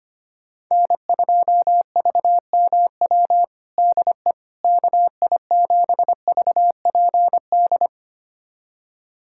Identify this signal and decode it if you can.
Morse code, heard as a keyed tone, about 25 wpm, 700 Hz. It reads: N2VMW DE KI7VPD
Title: USA Callsign DE Callsign 25wpm